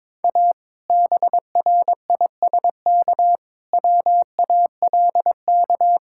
Morse code, heard as a keyed tone, about 22 wpm, 700 Hz. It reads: A BRISK WALK
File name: Difficult Sentences with Hints - 22wpm